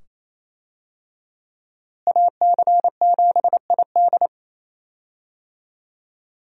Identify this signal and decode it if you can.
Morse code, heard as a keyed tone, about 28 wpm, 700 Hz. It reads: AC7ID
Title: USA Call Signs - 5 Characters - Repeated 2x - 28wpm